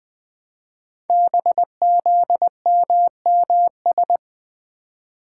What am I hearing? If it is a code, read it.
Morse code, heard as a keyed tone, about 20 wpm, 700 Hz. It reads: BZMMS
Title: Code Groups - 5 Letters - 20wpm